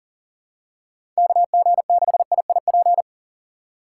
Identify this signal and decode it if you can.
Morse code, heard as a keyed tone, about 40 wpm, 700 Hz. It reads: KG6IIP
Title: K1USN SST QSO Practice - 40wpm